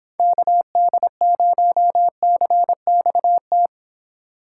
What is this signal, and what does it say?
Morse code, heard as a keyed tone, about 26 wpm, 700 Hz. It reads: KD0CXT